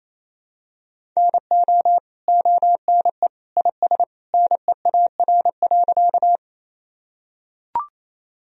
Morse code, heard as a keyed tone, about 28 wpm, 700 Hz. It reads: NO ONE IS NEAR.
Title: Sentences from Top 1000 Words - Review Part 1 - 28wpm